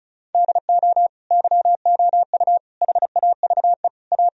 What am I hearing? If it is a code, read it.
Morse code, heard as a keyed tone, about 36 wpm, 700 Hz. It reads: DO YOU HAVE A
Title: Sentences from Top 1000 Words - Review Part 4 - 35wpm